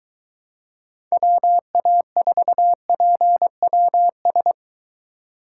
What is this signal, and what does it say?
Morse code, heard as a keyed tone, about 23 wpm, 700 Hz. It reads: WA4PWS